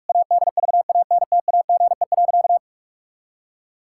Morse code, heard as a keyed tone, about 46 wpm, 700 Hz. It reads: ADVANTAGE.
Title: Sentences from Top 1000 Words - Review Part 6 - 45wpm